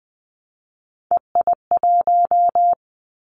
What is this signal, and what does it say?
Morse code, heard as a keyed tone, about 20 wpm, 700 Hz. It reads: EI1
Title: Callsign Prefixes for Top 50 DXCC - 20wpm